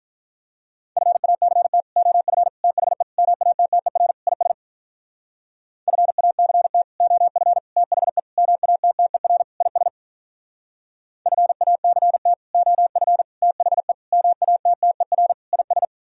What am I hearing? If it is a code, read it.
Morse code, heard as a keyed tone, about 46 wpm, 700 Hz. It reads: FACT OF THE MATTER IS FACT OF THE MATTER IS FACT OF THE MATTER IS